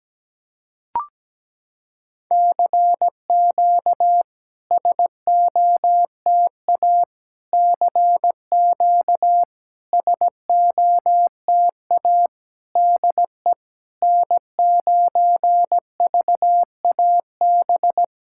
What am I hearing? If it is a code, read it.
Morse code, heard as a keyed tone, about 17 wpm, 700 Hz. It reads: CQ SOTA CQ SOTA DE N9VAB
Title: SOTA QSO Practice 17wpm